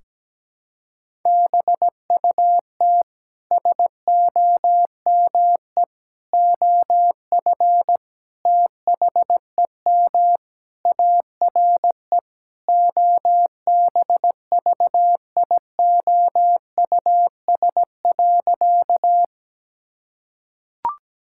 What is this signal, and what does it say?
Morse code, heard as a keyed tone, about 17 wpm, 700 Hz. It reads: BUT SOME OF THEM ARE OBVIOUS.
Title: Sentences from Top 1600 Words 17wpm